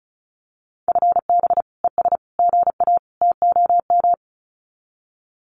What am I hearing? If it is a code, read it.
Morse code, heard as a keyed tone, about 34 wpm, 700 Hz. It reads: FB ES GA TOM
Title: Common QSO Phrases Encore 35wpm